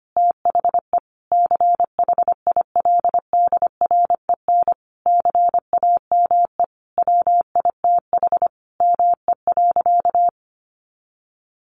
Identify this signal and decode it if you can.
Morse code, heard as a keyed tone, about 25 wpm, 700 Hz. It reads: THE CHILDREN CAME WITH ME.
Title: Sentences from Top 1300 Words 25wpm